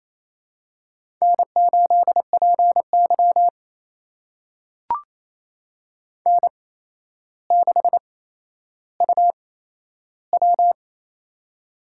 Morse code, heard as a keyed tone, about 28 wpm, spaced as for 12 wpm, 700 Hz. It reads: N8PY N6UW
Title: USA Call Signs - 4 Characters - 8x Character Spacing - 28wpm